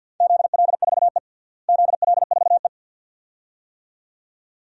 Morse code, heard as a keyed tone, about 50 wpm, 700 Hz. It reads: ZL4E ZL4E
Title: DXCC-100 Call Signs - 4 Characters - Repeated 2x - 50wpm